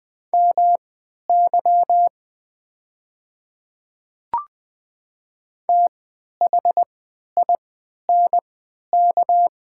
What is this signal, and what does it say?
Morse code, heard as a keyed tone, about 20 wpm, spaced as for 11 wpm, 690 Hz. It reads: MY THINK
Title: 3x Character Spacing - Top 300 Words - 20wpm